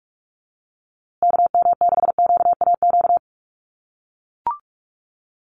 Morse code, heard as a keyed tone, about 44 wpm, 700 Hz. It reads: KM6QAQ
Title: USA Call Signs Repeated 3x 45wpm